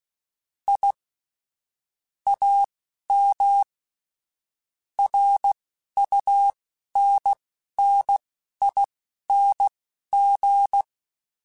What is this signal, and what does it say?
Morse code, heard as a keyed tone, about 16 wpm, spaced as for 11 wpm, 800 Hz. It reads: I AM RUNNING